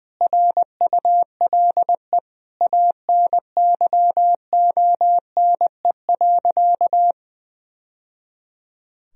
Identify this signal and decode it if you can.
Morse code, heard as a keyed tone, about 21 wpm, 700 Hz. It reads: RULE ANYONE.
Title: Sentences from Top 1000 Words - Review Part 2 - 20wpm